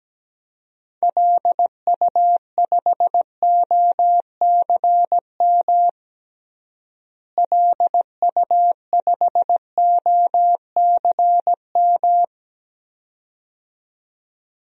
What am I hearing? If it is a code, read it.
Morse code, heard as a keyed tone, about 17 wpm, 700 Hz. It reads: LU5OCM LU5OCM
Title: DXCC-100 Call Signs - 6 Characters - Repeated 2x - 17wpm